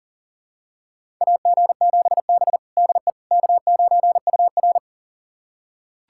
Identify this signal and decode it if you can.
Morse code, heard as a keyed tone, about 40 wpm, 700 Hz. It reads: AG7B DE K9UR
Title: USA Callsign DE Callsign - 4 Characters - 40wpm